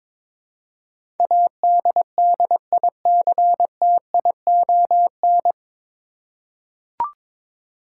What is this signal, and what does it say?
Morse code, heard as a keyed tone, about 22 wpm, 700 Hz. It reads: ADDICTION
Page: Morse Code Ninja practice 22wpm